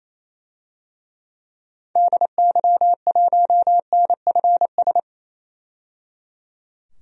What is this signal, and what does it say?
Morse code, heard as a keyed tone, about 28 wpm, 700 Hz. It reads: DY1NFS
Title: DXCC-100 Call Signs - 6 Characters - 28wpm